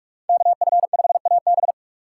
Morse code, heard as a keyed tone, about 47 wpm, 690 Hz. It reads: KF5AB
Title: USA Call Signs - 5 Characters - 2x Character Spacing - 45wpm